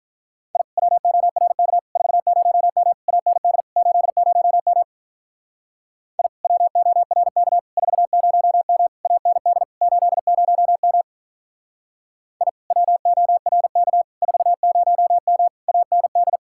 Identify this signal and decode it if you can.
Morse code, heard as a keyed tone, about 44 wpm, 700 Hz. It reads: I WORK 40M AND 80M I WORK 40M AND 80M I WORK 40M AND
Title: Speed-Racing Common QSO Phrases Encore - 35wpm